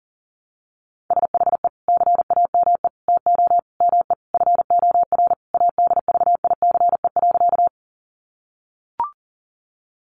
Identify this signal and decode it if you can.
Morse code, heard as a keyed tone, about 40 wpm, 700 Hz. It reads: SHE CAME TO ME FOR ADVICE.
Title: Sentences from Top 1100 Words 40wpm